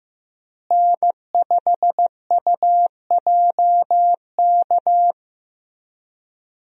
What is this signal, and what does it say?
Morse code, heard as a keyed tone, about 15 wpm, 700 Hz. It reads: N5UJK
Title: USA Call Signs - 5 Characters - Repeated 3x - 15wpm